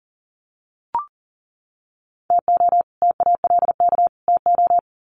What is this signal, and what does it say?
Morse code, heard as a keyed tone, about 40 wpm, 700 Hz. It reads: TO TALK TO
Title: Sets of 3 Words 40wpm